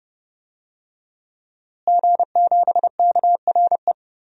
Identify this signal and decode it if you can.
Morse code, heard as a keyed tone, about 32 wpm, 700 Hz. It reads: G7KRE
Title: DXCC-100 Call Signs - 5 Characters - Repeated 3x - 30wpm